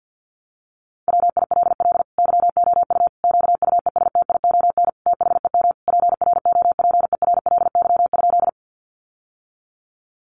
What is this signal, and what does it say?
Morse code, heard as a keyed tone, about 50 wpm, 700 Hz. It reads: WILL YOU QUESTION THEM PROPERLY?
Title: Sentences from Top 1100 Words 50wpm